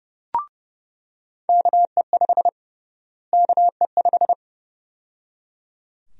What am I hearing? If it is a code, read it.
Morse code, heard as a keyed tone, about 32 wpm, 700 Hz. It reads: KE5 KE5
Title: USA Call Sign Prefixes - Repeated 2x - 30wpm